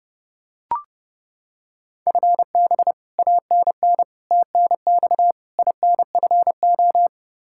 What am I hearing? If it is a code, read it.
Morse code, heard as a keyed tone, about 30 wpm, 700 Hz. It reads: FB ANN TNX INFO
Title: Practice QSO - Exchange 3 - More Info - 30wpm